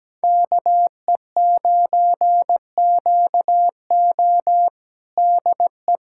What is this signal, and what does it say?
Morse code, heard as a keyed tone, about 17 wpm, 700 Hz. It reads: KE9QO DE
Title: USA Callsign DE Callsign - 5 Characters - 17wpm